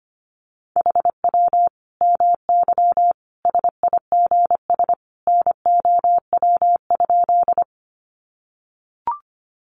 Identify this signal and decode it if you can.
Morse code, heard as a keyed tone, about 25 wpm, 700 Hz. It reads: HW MY SIGS NOW?
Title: Common QSO Phrases 25wpm